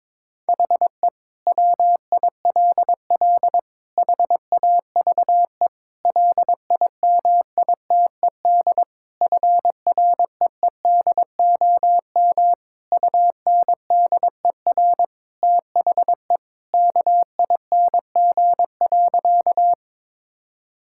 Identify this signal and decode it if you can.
Morse code, heard as a keyed tone, about 22 wpm, 700 Hz. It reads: HE WILL HAVE LIMITED FREEDOM UNDER THE KING.
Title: Sentences from Top 1000 Words - Review Part 6 - 22wpm